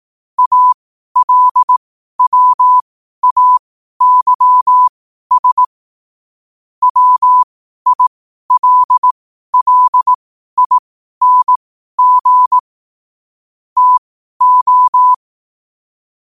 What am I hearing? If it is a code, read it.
Morse code, heard as a keyed tone, about 18 wpm, spaced as for 12 wpm, 1000 Hz. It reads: ALWAYS WILLING TO
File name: qotdfiles / 12wpm